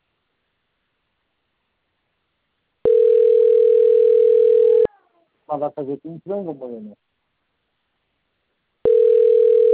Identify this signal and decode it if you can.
Ringback tone (North America), 440 + 480 Hz, with speech.
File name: rg-1600-unknown- - - .33470